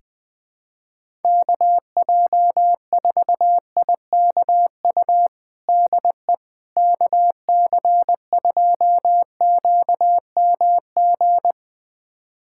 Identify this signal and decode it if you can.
Morse code, heard as a keyed tone, about 20 wpm, 700 Hz. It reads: KJ4IKU DE KC2QMG
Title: USA Callsign DE Callsign - 6 Characters - 20wpm